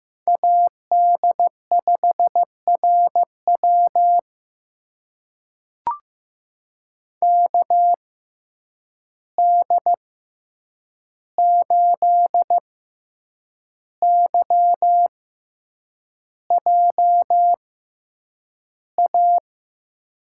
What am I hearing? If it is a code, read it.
Morse code, heard as a keyed tone, about 15 wpm, spaced as for 7 wpm, 700 Hz. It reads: AD5RW KD8YJA
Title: USA Call Signs - 6x Character Spacing - 15wpm